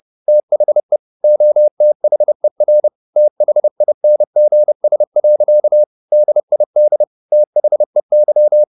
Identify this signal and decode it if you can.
Morse code, heard as a keyed tone, about 30 wpm, 600 Hz. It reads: THE OTHER THINGS. DID THEY